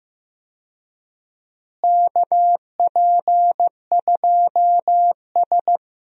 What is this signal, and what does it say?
Morse code, heard as a keyed tone, about 15 wpm, 700 Hz. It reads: KP2S
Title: DXCC-100 Call Signs - 4 Characters - Repeated 3x - 15wpm